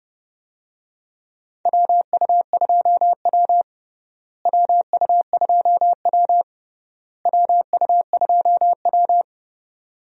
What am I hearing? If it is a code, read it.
Morse code, heard as a keyed tone, about 30 wpm, 700 Hz. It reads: WU2W WU2W WU2W
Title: USA Call Signs - 4 Characters - Repeated 3x - 30wpm